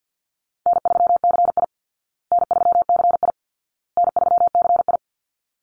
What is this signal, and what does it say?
Morse code, heard as a keyed tone, about 50 wpm, 700 Hz. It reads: N3CI N3CI N3CI
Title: Call Signs Repeated 2x - 50wpm